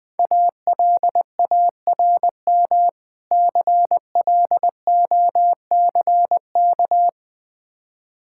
Morse code, heard as a keyed tone, about 20 wpm, 700 Hz. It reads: ALARM CLOCK
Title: Easy Words - Speed-Racing with Hints - 20wpm